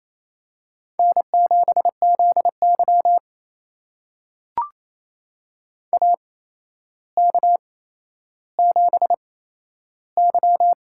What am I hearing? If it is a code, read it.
Morse code, heard as a keyed tone, about 28 wpm, spaced as for 12 wpm, 700 Hz. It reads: N7ZY AK7Y
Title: USA Call Signs - 4 Characters - 8x Character Spacing - 28wpm